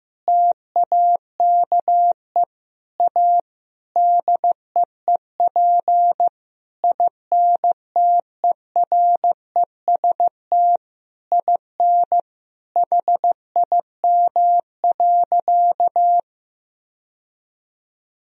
Morse code, heard as a keyed tone, about 15 wpm, 700 Hz. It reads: TAKE A DEEP INTEREST IN HIM.
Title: Sentences from Top 1000 Words - Review Part 3 - 15wpm